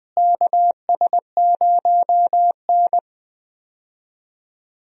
Morse code, heard as a keyed tone, about 20 wpm, 700 Hz. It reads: KS0N
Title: USA Call Signs - 4 Characters - Repeated 2x - 20wpm